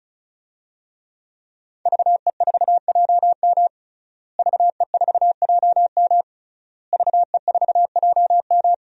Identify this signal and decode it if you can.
Morse code, heard as a keyed tone, about 36 wpm, 700 Hz. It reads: VE4JM VE4JM VE4JM
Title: DXCC-100 Call Signs - 5 Characters - Repeated 3x - 35wpm